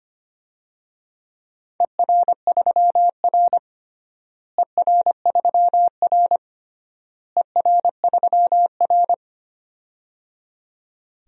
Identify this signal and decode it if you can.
Morse code, heard as a keyed tone, about 25 wpm, 700 Hz. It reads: ER3R ER3R ER3R
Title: DXCC-100 Call Signs - 4 Characters - Repeated 3x - 25wpm